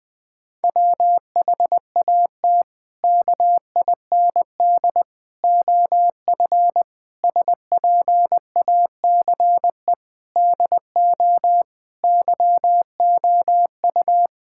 Morse code, heard as a keyed tone, about 20 wpm, 700 Hz. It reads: WHAT KIND OF SPACE DO YOU
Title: Sentences from Top 1000 Words - Review Part 5 - 20wpm